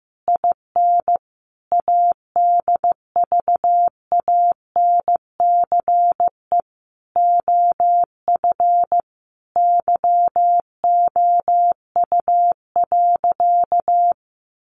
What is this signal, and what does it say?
Morse code, heard as a keyed tone, about 15 wpm, 700 Hz. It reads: IN ADVANCE OF YOU.
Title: Sentences from Top 1400 Words 15wpm